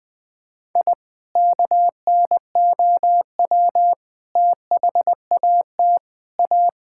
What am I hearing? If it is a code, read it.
Morse code, heard as a keyed tone, about 20 wpm, 700 Hz. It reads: I KNOW THAT A
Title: Sentences with 4-Word Phrases - 20wpm